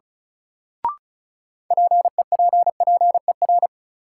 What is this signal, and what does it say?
Morse code, heard as a keyed tone, about 34 wpm, 700 Hz. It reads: PEPPER
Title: Sets of 5 Words - New Word Drill - 35wpm